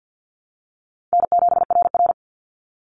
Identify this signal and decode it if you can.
Morse code, heard as a keyed tone, about 50 wpm, 700 Hz. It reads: N7RR
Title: Easy Call Signs - 50wpm